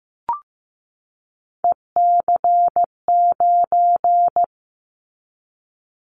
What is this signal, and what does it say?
Morse code, heard as a keyed tone, about 15 wpm, 700 Hz. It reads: EC9
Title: Callsign Prefixes for Top 100 DXCC - 15wpm